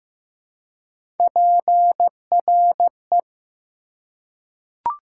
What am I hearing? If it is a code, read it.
Morse code, heard as a keyed tone, about 15 wpm, 700 Hz. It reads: PRE
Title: Prefixes and Suffixes 15wpm